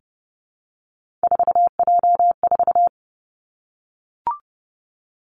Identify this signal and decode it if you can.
Morse code, heard as a keyed tone, about 30 wpm, 700 Hz. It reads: 4J4
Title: Three Challenging Letter-Number Combinations 30wpm